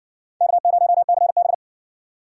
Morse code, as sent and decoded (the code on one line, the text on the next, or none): -.- ----- -.-. -...
K0CB